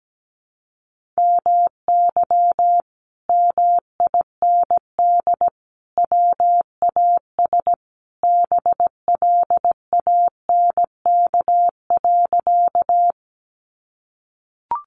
Morse code, heard as a keyed tone, about 17 wpm, 700 Hz. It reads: MY MIND WAS BLANK.
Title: Sentences from Top 1500 Words 17wpm